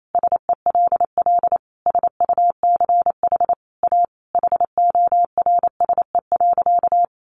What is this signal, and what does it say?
Morse code, heard as a keyed tone, about 28 wpm, 700 Hz. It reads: SELL SUCH A HORSE.
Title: Sentences from Top 1100 Words 28wpm